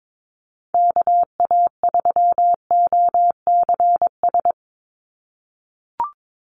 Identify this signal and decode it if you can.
Morse code, heard as a keyed tone, about 22 wpm, 700 Hz. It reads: KA3OCS
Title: Call Signs - 22wpm